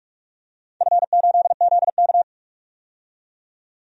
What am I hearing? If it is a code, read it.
Morse code, heard as a keyed tone, about 46 wpm, 700 Hz. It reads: F8ZK
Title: DXCC-100 Call Signs - 4 Characters - 45wpm